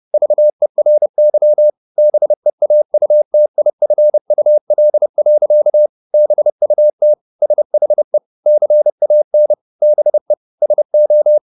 Morse code, heard as a keyed tone, about 30 wpm, 600 Hz. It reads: VERY BEAUTIFUL. BUT SHE CAN BE SO
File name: the old man and the sea / the old man and the sea 1 30wpm